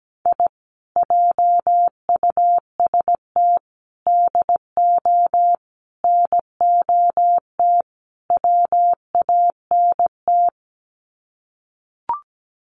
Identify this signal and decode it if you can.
Morse code, heard as a keyed tone, about 17 wpm, 700 Hz. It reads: I JUST DO NOT WANT
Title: Sets of 5 Words Encore2 17wpm